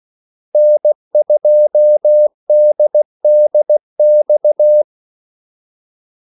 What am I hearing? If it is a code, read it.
Morse code, heard as a keyed tone, about 16 wpm, 600 Hz. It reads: N2DDX
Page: Kaldesignaler 16 wpm